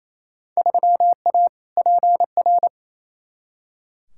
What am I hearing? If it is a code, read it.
Morse code, heard as a keyed tone, about 28 wpm, 700 Hz. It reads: 3A PR
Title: Field Day Exchanges Encore - 28wpm